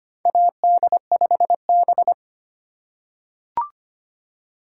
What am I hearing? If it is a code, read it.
Morse code, heard as a keyed tone, about 25 wpm, 700 Hz. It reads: AD5B
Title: USA Call Signs - 4 Characters - 4x Character Spacing - 25wpm